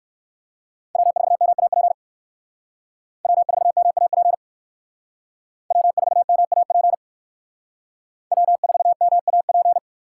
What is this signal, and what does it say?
Morse code, heard as a keyed tone, about 58 wpm, 700 Hz. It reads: W4MAP W4MAP W4MAP W4MAP